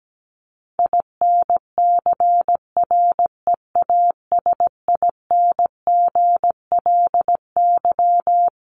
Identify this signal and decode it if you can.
Morse code, heard as a keyed tone, about 17 wpm, 700 Hz. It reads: INCREASINGLY
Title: LY Suffix - Top 100 Words with Base Words 17wpm